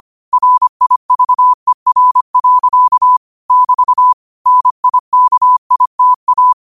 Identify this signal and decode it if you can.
Morse code, heard as a keyed tone, about 25 wpm, 1000 Hz. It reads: RIVER. <BT> NIKITA
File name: qotdfiles / 25wpm